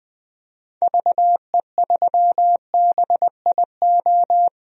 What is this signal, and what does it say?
Morse code, heard as a keyed tone, about 21 wpm, 700 Hz. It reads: VE3BIO
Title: DXCC-100 Call Signs - 6 Characters - 20wpm